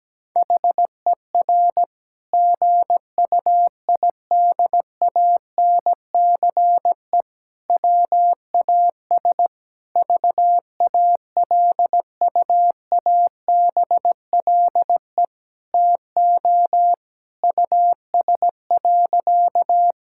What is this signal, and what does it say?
Morse code, heard as a keyed tone, about 17 wpm, 700 Hz. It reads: HER GUIDANCE WAS VALUABLE TO US.
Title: Sentences from Top 1600 Words 17wpm